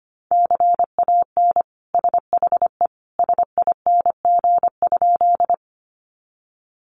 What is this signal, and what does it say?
Morse code, heard as a keyed tone, about 25 wpm, 700 Hz. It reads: CAN SHE SING?
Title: Sentences from Top 400 Words 25wpm